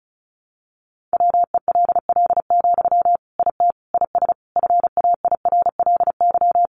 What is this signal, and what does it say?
Morse code, heard as a keyed tone, about 34 wpm, 700 Hz. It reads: WELL, IT IS FAIRLY
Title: Sentences from Top 1100 Words 35wpm